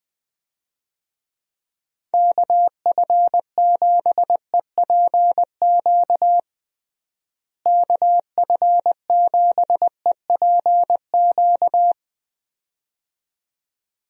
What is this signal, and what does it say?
Morse code, heard as a keyed tone, about 20 wpm, 700 Hz. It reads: KF7EPQ KF7EPQ
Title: USA Call Signs - 6 Characters - Repeated 2x - 20wpm